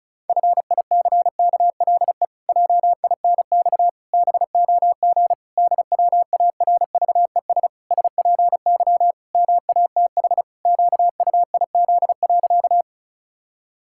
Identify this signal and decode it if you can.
Morse code, heard as a keyed tone, about 34 wpm, 700 Hz. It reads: FICKLE JINX BOG DWARVES SPY MATH QUIZ.
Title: Pangram Challenge v2 - 35wpm